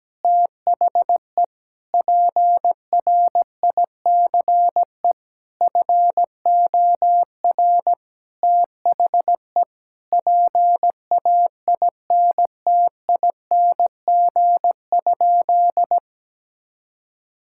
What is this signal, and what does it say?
Morse code, heard as a keyed tone, about 17 wpm, 700 Hz. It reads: THE PRICE FOR THE PAINTING?
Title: Sentences from Top 1000 Words - Review Part 6 - 17wpm